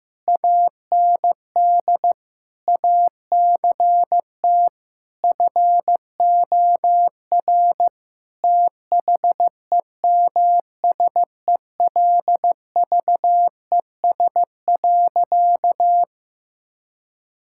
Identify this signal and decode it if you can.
Morse code, heard as a keyed tone, about 15 wpm, 700 Hz. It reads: AND ACT FOR THEMSELVES.
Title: Sentences from Top 1000 Words - Review Part 5 - 15wpm